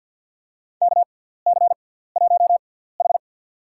Morse code, heard as a keyed tone, about 47 wpm, spaced as for 17 wpm, 700 Hz. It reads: KC1H
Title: USA Call Signs - 6x Character Spacing - 50wpm